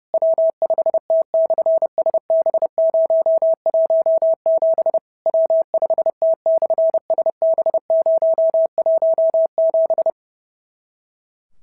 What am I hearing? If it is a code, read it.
Morse code, heard as a keyed tone, about 30 wpm, 650 Hz. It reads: W5T/SB017 W5T/SB017
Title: SOTA QSO Practice 30wpm